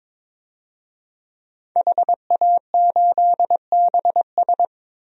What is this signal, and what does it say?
Morse code, heard as a keyed tone, about 22 wpm, 700 Hz. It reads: HA8BS
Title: DXCC-100 Call Signs - 5 Characters - Repeated 2x - 22wpm